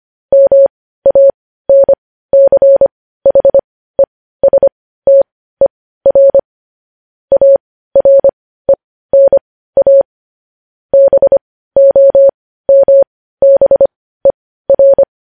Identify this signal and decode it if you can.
Morse code, heard as a keyed tone, about 25 wpm, spaced as for 15 wpm, 550 Hz. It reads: MANCHESTER ARENA BOMBER
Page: News Headlines in Morse Code at 15 WPM